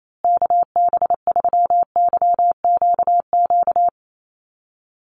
Morse code, heard as a keyed tone, about 28 wpm, 700 Hz. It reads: KB3YQQ
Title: USA Call Signs Repeated 3x 28wpm